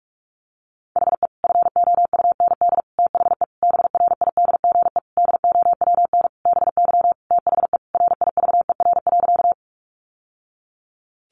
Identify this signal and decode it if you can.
Morse code, heard as a keyed tone, about 44 wpm, 700 Hz. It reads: HE FOUND THE BRIDGE DOWN BY THE RIVER.
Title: Sentences from Top 1200 Words 45wpm